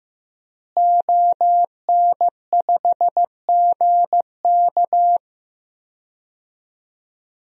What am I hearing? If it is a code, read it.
Morse code, heard as a keyed tone, about 16 wpm, 700 Hz. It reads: ON5GK
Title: DXCC-100 Call Signs - 5 Characters - Repeated 3x - 15wpm